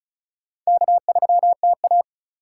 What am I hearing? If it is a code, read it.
Morse code, heard as a keyed tone, about 34 wpm, 690 Hz. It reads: K3TA
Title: USA Call Signs - 3x Character Spacing - 35wpm